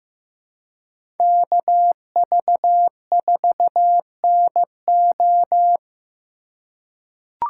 Morse code, heard as a keyed tone, about 15 wpm, 700 Hz. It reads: KV4NO
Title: USA Call Signs - 5 Characters - 4x Character Spacing - 15wpm